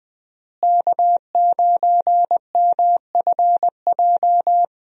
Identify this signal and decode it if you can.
Morse code, heard as a keyed tone, about 21 wpm, 700 Hz. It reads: K9MFJ
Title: USA Call Signs - 5 Characters - Repeated 3x - 20wpm